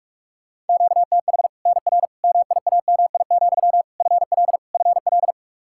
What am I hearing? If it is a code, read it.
Morse code, heard as a keyed tone, about 44 wpm, 700 Hz. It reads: QTH NR MIAMI, FL FL
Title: Practice QSO - Exchange 1 - Beginning - 45wpm